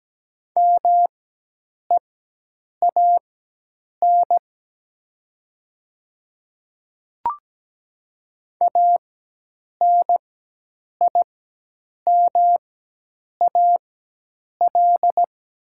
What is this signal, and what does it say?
Morse code, heard as a keyed tone, about 17 wpm, spaced as for 8 wpm, 700 Hz. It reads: MEAN ANIMAL
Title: 4x Character Spacing - Top 500 Words - 17wpm